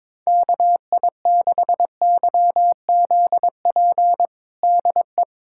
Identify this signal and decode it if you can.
Morse code, heard as a keyed tone, about 22 wpm, 700 Hz. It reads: KI6YZP DE
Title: USA Callsign DE Callsign 22wpm